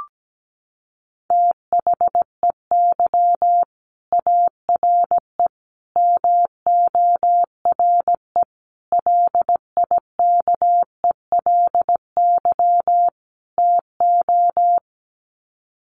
Morse code, heard as a keyed tone, about 17 wpm, 700 Hz. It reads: THEY ARE MORE LIKELY TO
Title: Sets of 5 Words Encore2 17wpm